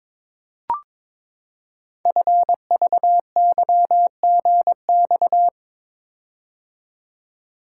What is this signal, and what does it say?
Morse code, heard as a keyed tone, about 21 wpm, 700 Hz. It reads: FVYGX
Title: Code Groups - 5 Letters - 22wpm